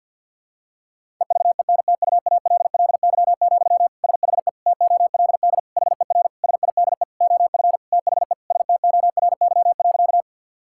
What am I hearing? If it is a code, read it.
Morse code, heard as a keyed tone, about 46 wpm, 700 Hz. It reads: EVENTUALLY, SHE TOLD HER SIDE OF THE STORY.
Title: Sentences from Top 1000 Words - Review Part 6 - 50wpm